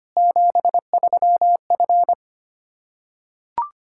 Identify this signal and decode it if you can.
Morse code, heard as a keyed tone, about 25 wpm, 700 Hz. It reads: 73F
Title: Practice QSO - Exchange 2 - Info - 25wpm